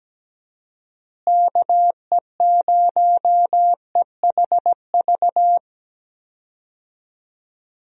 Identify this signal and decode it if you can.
Morse code, heard as a keyed tone, about 17 wpm, 700 Hz. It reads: KE0EHV
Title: USA Call Signs - 6 Characters - Repeated 2x - 17wpm